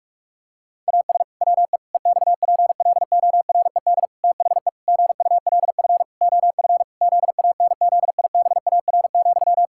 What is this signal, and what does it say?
Morse code, heard as a keyed tone, about 41 wpm, 700 Hz. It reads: AS WE EXPLORED THE GULF OF ZANZIBAR,
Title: Pangram Challenge v2 - 45wpm